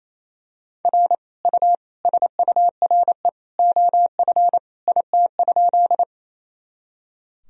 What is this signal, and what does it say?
Morse code, heard as a keyed tone, about 28 wpm, 700 Hz. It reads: R U SURE OF IT?
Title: Sentences from Top 300 Words and Abbreviations 28wpm